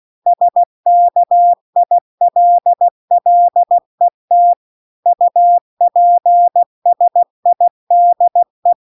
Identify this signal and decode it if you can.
Morse code, heard as a keyed tone, about 16 wpm, 700 Hz. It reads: SKILLET UPSIDE
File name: CW-The-Big-Twoheaded-River-2-16WPM